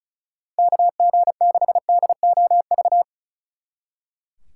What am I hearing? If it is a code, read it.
Morse code, heard as a keyed tone, about 45 wpm, 700 Hz. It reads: KG6DOV